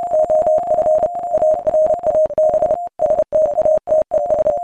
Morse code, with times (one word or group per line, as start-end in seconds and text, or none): 0.1-1.1 OF
1.3-2.7 ARAB
3.0-4.5 STATES